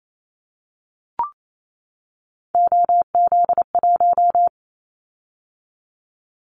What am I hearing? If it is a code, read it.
Morse code, heard as a keyed tone, about 28 wpm, 700 Hz. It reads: OZ1
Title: Callsign Prefixes for Top 100 DXCC - 28wpm